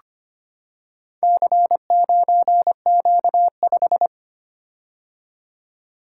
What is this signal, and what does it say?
Morse code, heard as a keyed tone, about 25 wpm, 700 Hz. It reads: C9Q5
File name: practice / 25wpm / Four Letter-Number Combinations - 25wpm